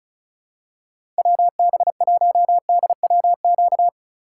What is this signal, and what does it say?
Morse code, heard as a keyed tone, about 36 wpm, 700 Hz. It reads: WB1DWQ
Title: USA Call Signs - 6 Characters - 3x Character Spacing - 35wpm